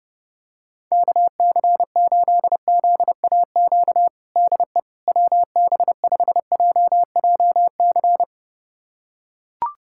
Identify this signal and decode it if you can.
Morse code, heard as a keyed tone, about 30 wpm, 700 Hz. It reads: KC8ZAQ DE WB5JJC
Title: USA Callsign DE Callsign 30wpm